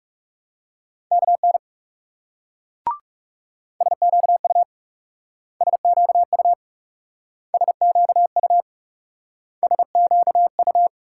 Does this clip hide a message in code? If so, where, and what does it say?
Yes: Morse code from 1.1 to 10.9 s, heard as a keyed tone, about 38 wpm, 700 Hz. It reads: KN SQU SQU SQU SQU